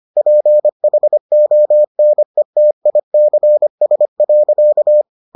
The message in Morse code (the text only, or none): PHONETICS.